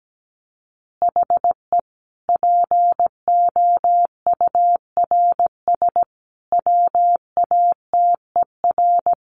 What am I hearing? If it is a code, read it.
Morse code, heard as a keyed tone, about 17 wpm, 700 Hz. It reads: HE POURS WATER
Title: Additional Easy Phrases 17wpm